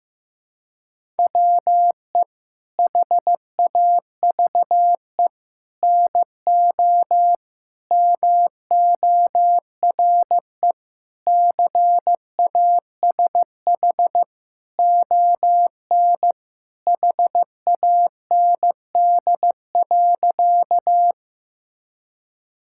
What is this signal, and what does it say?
Morse code, heard as a keyed tone, about 15 wpm, 700 Hz. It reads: WE HAVE NO MORE CASH ON HAND.
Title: Sentences from Top 1000 Words - Review Part 5 - 15wpm